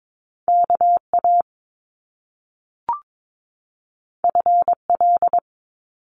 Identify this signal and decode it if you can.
Morse code, heard as a keyed tone, about 22 wpm, 700 Hz. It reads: KA FL
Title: Top Two Letter Combinations in Words 22wpm